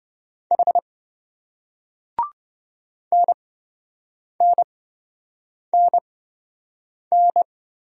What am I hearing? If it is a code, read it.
Morse code, heard as a keyed tone, about 27 wpm, spaced as for 7 wpm, 700 Hz. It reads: H NNNN